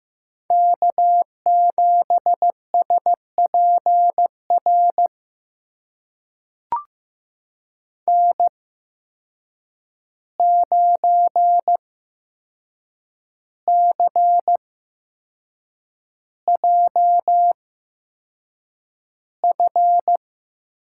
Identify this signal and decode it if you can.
Morse code, heard as a keyed tone, about 15 wpm, spaced as for 6 wpm, 700 Hz. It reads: K7SPR N9CJF
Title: USA Call Signs - 5 Characters - 8x Character Spacing - 15wpm